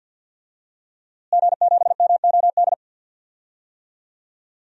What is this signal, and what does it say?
Morse code, heard as a keyed tone, about 46 wpm, 700 Hz. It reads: G7MOD
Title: DXCC-100 Call Signs - 5 Characters - Repeated 3x - 50wpm